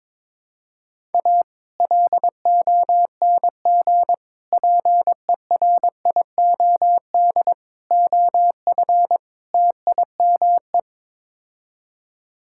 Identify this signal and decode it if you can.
Morse code, heard as a keyed tone, about 22 wpm, 700 Hz. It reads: A LONG PERIOD OF TIME
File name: practice / 22wpm / Build A Phrase - 5-Word Phrases - 22wpm